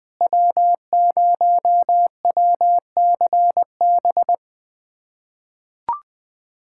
Morse code, heard as a keyed tone, about 20 wpm, 700 Hz. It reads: W0WCB
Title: USA Callsign DE Callsign 20wpm